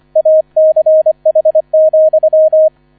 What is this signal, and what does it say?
Morse code, heard as a keyed tone, about 25 wpm, 630 Hz, with radio noise. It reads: ACH,